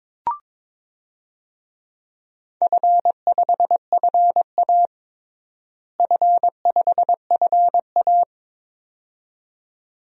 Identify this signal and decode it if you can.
Morse code, heard as a keyed tone, about 23 wpm, 700 Hz. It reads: F5FA F5FA
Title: DXCC-100 Call Signs - 4 Characters - Repeated 2x - 22wpm